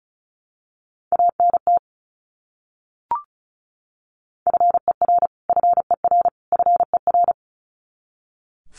Morse code, heard as a keyed tone, about 34 wpm, 700 Hz. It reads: ANT FER FER FER
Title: Abbreviations Repeated 3x - 35wpm